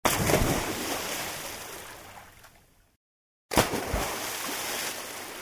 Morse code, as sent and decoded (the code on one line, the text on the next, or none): none